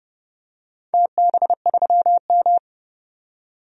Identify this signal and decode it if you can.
Morse code, heard as a keyed tone, about 28 wpm, 700 Hz. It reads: TB3M
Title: DXCC-100 Call Signs - 4 Characters - 30wpm